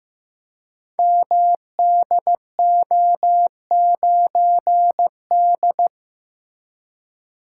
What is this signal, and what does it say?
Morse code, heard as a keyed tone, about 15 wpm, 700 Hz. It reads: MDO9D
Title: Code Groups - 5 Letter:Numbers - 15wpm